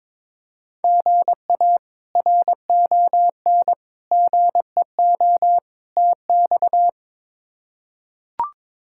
Morse code, heard as a keyed tone, about 22 wpm, 700 Hz. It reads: GA RON GEO TX
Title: K1USN SST QSO Practice - 22wpm